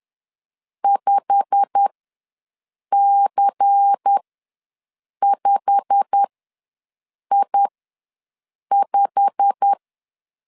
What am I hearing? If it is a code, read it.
Morse code, heard as a keyed tone, about 10 wpm, spaced as for 6 wpm, 800 Hz. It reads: 5C5I5
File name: LEZIONI / 8 YJ45 / 8 YJ45 ws 10wpm 3843